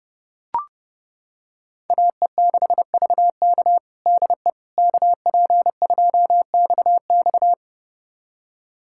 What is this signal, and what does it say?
Morse code, heard as a keyed tone, about 31 wpm, 700 Hz. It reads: AE6VK DE KP2XX
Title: USA Callsign DE Callsign - 5 Characters - 30wpm